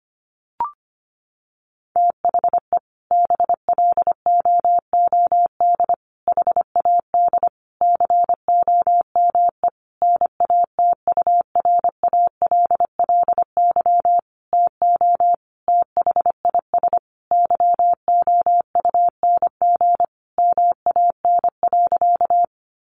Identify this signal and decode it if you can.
Morse code, heard as a keyed tone, about 25 wpm, 700 Hz. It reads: THE BLOOD HAD COME NATURALLY TO THIS YOUNG MAN.
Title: Sentences from Top 1300 Words 25wpm